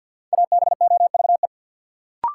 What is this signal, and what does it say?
Morse code, heard as a keyed tone, about 53 wpm, 700 Hz. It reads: ABOVE
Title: Sentences from Top 1000 Words - Review Part 2 - New Word Drill - 50wpm